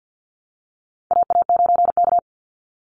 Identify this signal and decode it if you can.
Morse code, heard as a keyed tone, about 48 wpm, 700 Hz. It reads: AA9K
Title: Call Signs Repeated 3x - 50wpm